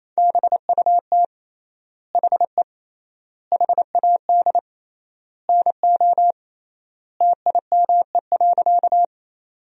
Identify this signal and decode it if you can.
Morse code, heard as a keyed tone, about 28 wpm, 700 Hz. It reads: BUT HE HAD NO TIME.
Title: 3x Word-Spacing - Sentences from Top 100 Words - 28wpm